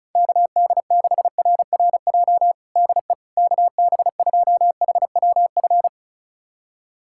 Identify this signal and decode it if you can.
Morse code, heard as a keyed tone, about 36 wpm, 700 Hz. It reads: KD6RRJ DE KB2HWF
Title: USA Callsign DE Callsign - 6 Characters - 35wpm